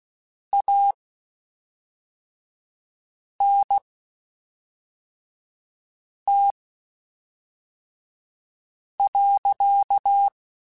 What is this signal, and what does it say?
Morse code, heard as a keyed tone, about 16 wpm, spaced as for 3 wpm, 800 Hz. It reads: ANT.
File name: QSOs / QSOs-mp3s / WD0FBT 1@5